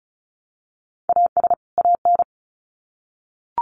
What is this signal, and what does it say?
Morse code, heard as a keyed tone, about 34 wpm, 700 Hz. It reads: AS AN
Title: Sets of 2 Words 35wpm